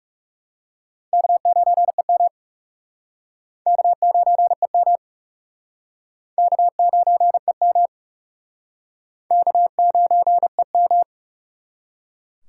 Morse code, heard as a keyed tone, about 41 wpm, 700 Hz. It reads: K9EM K9EM K9EM K9EM